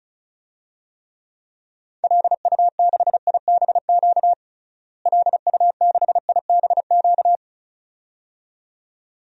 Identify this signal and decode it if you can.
Morse code, heard as a keyed tone, about 34 wpm, 700 Hz. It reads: LU6IBQ LU6IBQ
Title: DXCC-100 Call Signs - 6 Characters - Repeated 2x - 35wpm